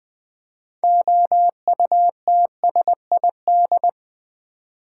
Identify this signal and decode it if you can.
Morse code, heard as a keyed tone, about 20 wpm, 700 Hz. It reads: OUTSID
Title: Build A Word - 2 Letters At A Time - 8-Letter Words - 20wpm